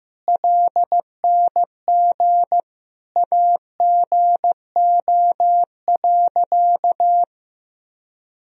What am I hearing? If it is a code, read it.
Morse code, heard as a keyed tone, about 15 wpm, 700 Hz. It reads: LNG AGO.
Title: Sentences from Top 500 Words and Abbreviations 15wpm